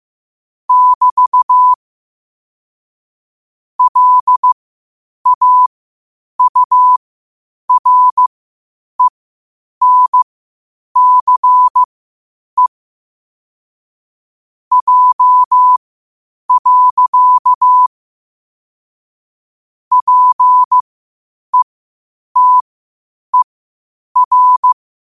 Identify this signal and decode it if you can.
Morse code, heard as a keyed tone, about 15 wpm, spaced as for 8 wpm, 1000 Hz. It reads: <BT> LAURENCE J. PETER
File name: qotdfiles / 8wpm